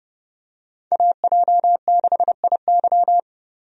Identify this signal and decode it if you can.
Morse code, heard as a keyed tone, about 30 wpm, 700 Hz. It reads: AJ6IY
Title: USA Call Signs - 5 Characters - 30wpm